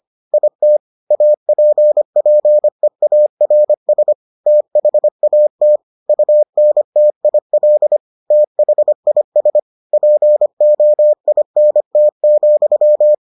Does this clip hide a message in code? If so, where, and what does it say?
Yes: Morse code from 0.3 to 13.1 s, heard as a keyed tone, about 25 wpm, 600 Hz. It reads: IT APPEARS THAT UNTIL THIS POINT,